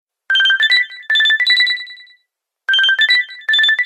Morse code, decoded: none